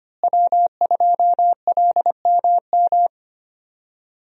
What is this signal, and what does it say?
Morse code, heard as a keyed tone, about 25 wpm, 700 Hz. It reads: W2LMM
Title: USA Call Signs - 5 Characters - Repeated 2x - 25wpm